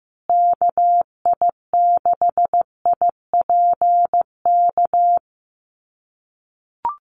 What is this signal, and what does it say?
Morse code, heard as a keyed tone, about 15 wpm, 700 Hz. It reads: KI6IPK
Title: USA Call Signs Repeated 2x 15wpm